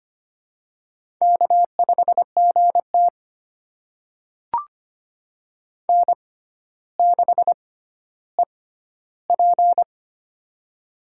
Morse code, heard as a keyed tone, about 25 wpm, spaced as for 13 wpm, 700 Hz. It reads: K5GT N6EP
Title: USA Call Signs - 4 Characters - 6x Character Spacing - 25wpm